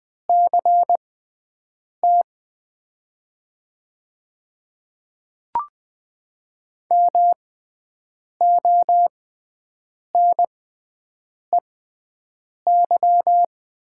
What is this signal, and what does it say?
Morse code, heard as a keyed tone, about 20 wpm, spaced as for 7 wpm, 700 Hz. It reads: CT MONEY
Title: 6x Character Spacing - Top 500 Words - 20wpm